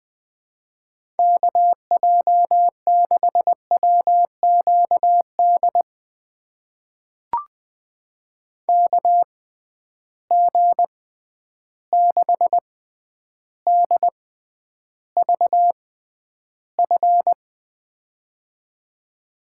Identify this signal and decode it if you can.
Morse code, heard as a keyed tone, about 21 wpm, spaced as for 10 wpm, 700 Hz. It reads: KJ6WQD KG6DVF
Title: USA Call Signs - 6 Characters - 6x Character Spacing - 20wpm